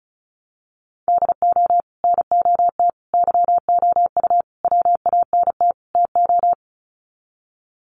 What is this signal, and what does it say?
Morse code, heard as a keyed tone, about 34 wpm, 700 Hz. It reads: DO NOT YOU WANT TO
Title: Sets of 5 Words Encore2 35wpm